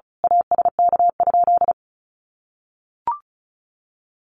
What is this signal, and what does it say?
Morse code, heard as a keyed tone, about 34 wpm, 700 Hz. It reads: ASK?
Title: Sentences from Top 200 Words 35wpm